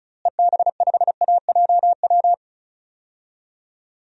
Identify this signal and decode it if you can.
Morse code, heard as a keyed tone, about 36 wpm, 700 Hz. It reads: EB5AJW
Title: DXCC-100 Call Signs - 6 Characters - Repeated 3x - 35wpm